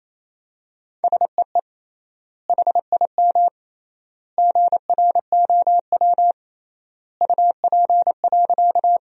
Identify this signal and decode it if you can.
Morse code, heard as a keyed tone, about 28 wpm, 700 Hz. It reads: SEE HIM GROW UP.
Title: 3x Word-Spacing - Sentences from Top 300 Words - 28wpm